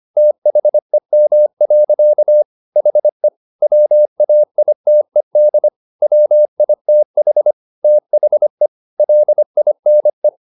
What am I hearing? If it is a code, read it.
Morse code, heard as a keyed tone, about 25 wpm, 600 Hz. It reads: THEM. HE WAITED WITH THE LINE
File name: the old man and the sea part 1 25wpm